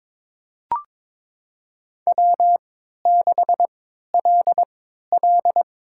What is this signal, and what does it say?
Morse code, heard as a keyed tone, about 22 wpm, spaced as for 13 wpm, 700 Hz. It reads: W6LL
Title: USA Call Signs - 4 Characters - 3x Character Spacing - 22wpm